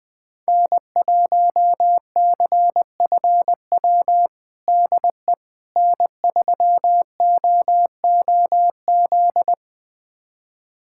Morse code, heard as a keyed tone, about 20 wpm, 700 Hz. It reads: N1CFW DE N3OOZ
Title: USA Callsign DE Callsign 20wpm